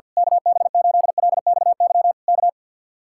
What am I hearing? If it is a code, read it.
Morse code, heard as a keyed tone, about 49 wpm, 700 Hz. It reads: KB8LXY K
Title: Speed-Racing Common QSO Phrases - 50wpm